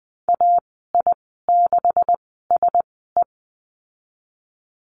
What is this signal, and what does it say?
Morse code, heard as a keyed tone, about 20 wpm, spaced as for 14 wpm, 700 Hz. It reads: AI6SE
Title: 154 Lesson 25g - Letter p - Call Signs 20wpm